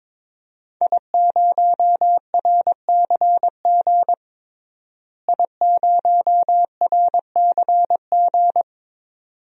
Morse code, heard as a keyed tone, about 22 wpm, 700 Hz. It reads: I0RCG I0RCG
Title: DXCC-100 Call Signs - 5 Characters - Repeated 2x - 22wpm